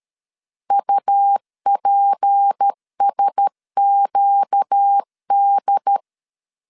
Morse code, heard as a keyed tone, about 13 wpm, 800 Hz. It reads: UPSQD
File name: LEZIONI / 7 GF9Z / 7 GF9 12wpm 2319